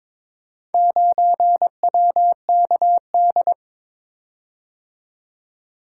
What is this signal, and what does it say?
Morse code, heard as a keyed tone, about 22 wpm, 700 Hz. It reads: 9WKD
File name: practice / 22wpm / Four Letter-Number Combinations - 22wpm